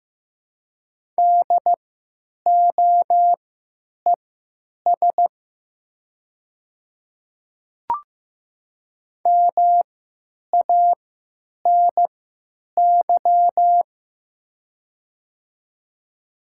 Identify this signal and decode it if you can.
Morse code, heard as a keyed tone, about 15 wpm, spaced as for 9 wpm, 700 Hz. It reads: DOES MANY
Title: 3x Character Spacing - Top 200 Words - 15wpm